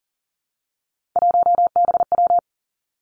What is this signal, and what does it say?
Morse code, heard as a keyed tone, about 38 wpm, 700 Hz. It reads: 1BW
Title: Three Challenging Letter-Number Combinations 40wpm